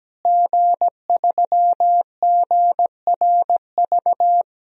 Morse code, heard as a keyed tone, about 17 wpm, 700 Hz. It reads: G3GRV
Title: DXCC-100 Call Signs - 5 Characters - Repeated 3x - 17wpm